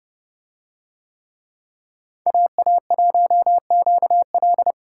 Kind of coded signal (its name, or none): Morse code